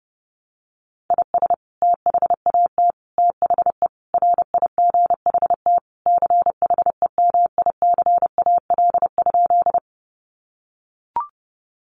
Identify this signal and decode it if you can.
Morse code, heard as a keyed tone, about 30 wpm, 700 Hz. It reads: IS THAT THE RIGHT CHEMICAL?
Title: Sentences from Top 1200 Words 30wpm